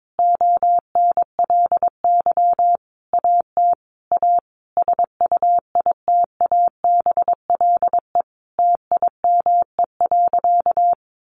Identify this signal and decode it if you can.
Morse code, heard as a keyed tone, about 22 wpm, 700 Hz. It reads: ONLY AT A SUITABLE TIME.
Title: Sentences from Top 1400 Words 22wpm